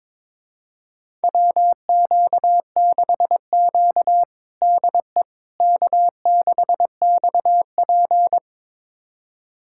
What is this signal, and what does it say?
Morse code, heard as a keyed tone, about 22 wpm, 700 Hz. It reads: WQ6Q DE K6XP
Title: USA Callsign DE Callsign - 4 Characters - 22wpm